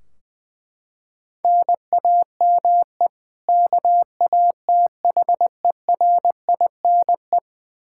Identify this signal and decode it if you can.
Morse code, heard as a keyed tone, about 20 wpm, 700 Hz. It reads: NAME KATHERINE
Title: Practice QSO - Exchange 1 - Beginning - 20wpm